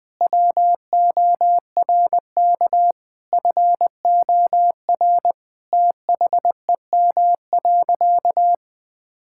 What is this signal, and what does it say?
Morse code, heard as a keyed tone, about 20 wpm, 700 Hz. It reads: WORK FOR THEM.
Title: Sentences from Top 1000 Words - Review Part 5 - 20wpm